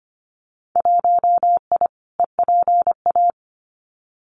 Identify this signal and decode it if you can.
Morse code, heard as a keyed tone, about 25 wpm, 700 Hz. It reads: 1I EPA
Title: Winter Field Day Exchanges Encore 25wpm